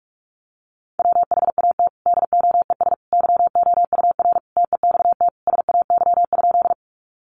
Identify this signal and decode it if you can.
Morse code, heard as a keyed tone, about 44 wpm, 700 Hz. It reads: WHAT DOES YOUR TEXT SAY?
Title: Sentences from Top 1200 Words 45wpm